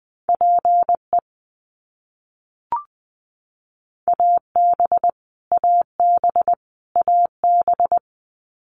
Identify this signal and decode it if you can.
Morse code, heard as a keyed tone, about 20 wpm, 700 Hz. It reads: PE AB AB AB
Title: Canadian Provinces and Territories - Repeated 3x - 20wpm